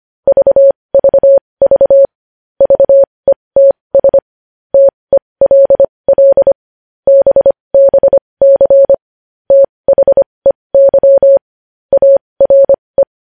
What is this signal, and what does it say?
Morse code, heard as a keyed tone, about 25 wpm, spaced as for 20 wpm, 550 Hz. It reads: VVV VETS TELL BBC THEY ARE
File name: podcast / News headlines-1x-25x20